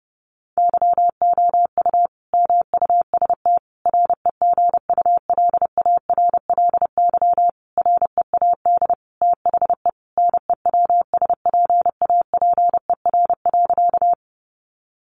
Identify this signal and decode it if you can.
Morse code, heard as a keyed tone, about 30 wpm, 700 Hz. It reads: YOU MUST REGULARLY READ THE NEWSPAPER.
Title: Sentences from Top 1400 Words 30wpm